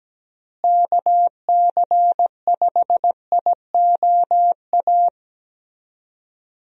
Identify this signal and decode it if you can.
Morse code, heard as a keyed tone, about 17 wpm, 700 Hz. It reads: KC5IOA
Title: USA Call Signs - 6 Characters - Repeated 3x - 17wpm